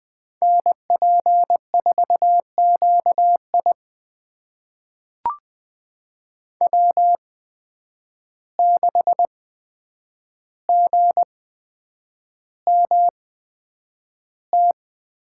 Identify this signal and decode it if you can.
Morse code, heard as a keyed tone, about 20 wpm, spaced as for 9 wpm, 700 Hz. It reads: NP4QI W6GMT
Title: USA Call Signs - 5 Characters - 8x Character Spacing - 20wpm